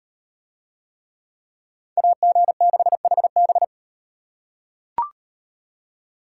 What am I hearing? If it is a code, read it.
Morse code, heard as a keyed tone, about 38 wpm, 700 Hz. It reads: AG6HB